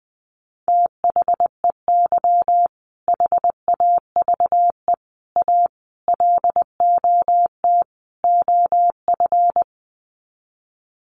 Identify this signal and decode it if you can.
Morse code, heard as a keyed tone, about 20 wpm, 700 Hz. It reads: THEY HAVE A LOT OF
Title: Sets of 5 Words Encore2 20wpm